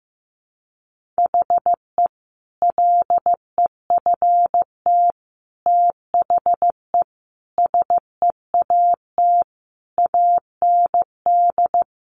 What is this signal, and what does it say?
Morse code, heard as a keyed tone, about 15 wpm, 700 Hz. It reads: HE LEFT THE SEAT AND
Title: Sentences from Top 1500 Words 15wpm